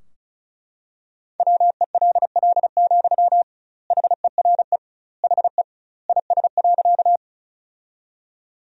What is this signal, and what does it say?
Morse code, heard as a keyed tone, about 36 wpm, 700 Hz. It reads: WELL, HERE HE IS.
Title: 2x Word-Spacing - Sentences from Top 200 Words - 35wpm